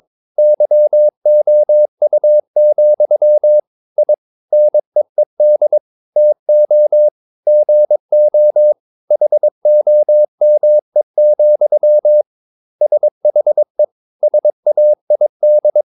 Morse code, heard as a keyed tone, about 22 wpm, 600 Hz. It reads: YOU, I NEED TO GO HOME, SHE SAID
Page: Romanen Never af Ken Follet på engelsk.